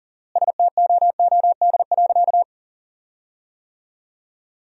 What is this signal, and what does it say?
Morse code, heard as a keyed tone, about 37 wpm, 700 Hz. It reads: STOOD.
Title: 4x Word-Spacing - Sentences from Top 500 Words - 40wpm